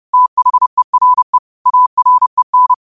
Morse code, heard as a keyed tone, about 30 wpm, 1000 Hz. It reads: THERE AREN
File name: qotdfiles / 30wpm